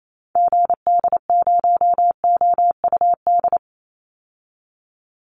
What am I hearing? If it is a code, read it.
Morse code, heard as a keyed tone, about 28 wpm, 700 Hz. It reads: GD0OUD
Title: Call Signs Repeated 3x - 28wpm